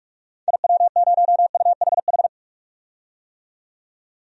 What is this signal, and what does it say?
Morse code, heard as a keyed tone, about 46 wpm, 700 Hz. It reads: IW0UHH
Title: DXCC-100 Call Signs - 6 Characters - 45wpm